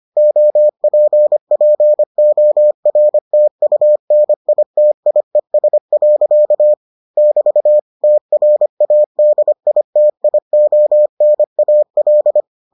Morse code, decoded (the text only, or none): OPPORTUNITIES. <BT> TRADITIONAL